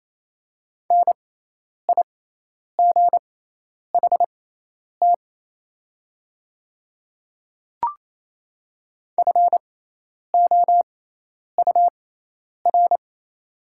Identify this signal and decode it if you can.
Morse code, heard as a keyed tone, about 28 wpm, spaced as for 10 wpm, 700 Hz. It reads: NIGHT FOUR
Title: 6x Character Spacing - Top 300 Words - 28wpm